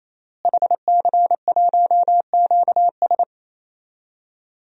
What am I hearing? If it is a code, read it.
Morse code, heard as a keyed tone, about 28 wpm, 700 Hz. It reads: HC1QS
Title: DXCC-100 Call Signs - 5 Characters - Repeated 3x - 28wpm